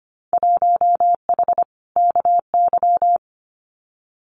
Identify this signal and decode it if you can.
Morse code, heard as a keyed tone, about 25 wpm, 700 Hz. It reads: 1H KY
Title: Winter Field Day Exchanges 25wpm